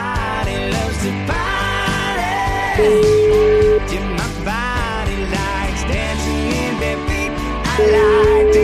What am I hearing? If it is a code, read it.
Ringback tone (Europe, 425 Hz), with noise.